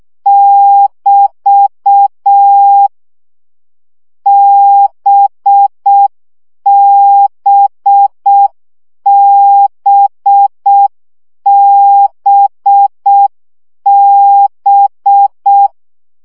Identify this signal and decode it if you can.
Morse code, heard as a keyed tone, about 6 wpm, 800 Hz. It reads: <BT> BBBBB
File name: CW gyakorlas b 5 baud